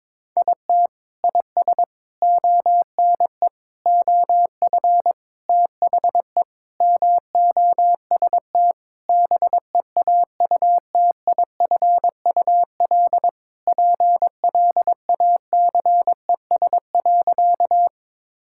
Morse code, heard as a keyed tone, about 21 wpm, 700 Hz. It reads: IT IS ONE OF THE MOST BEAUTIFUL PLACES.
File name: practice / 22wpm / Sentences with 4-Word Phrases - 22wpm